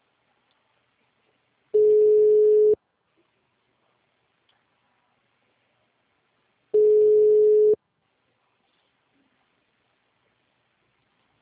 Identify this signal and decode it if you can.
Ringback tone (Europe, 425 Hz).